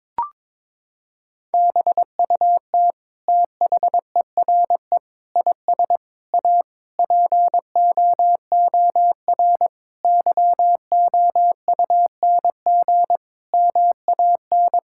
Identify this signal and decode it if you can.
Morse code, heard as a keyed tone, about 22 wpm, 700 Hz. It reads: BUT THERE IS A POOR YOUNG MAN
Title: Sentences from Top 1000 Words - Review Part 6 - 22wpm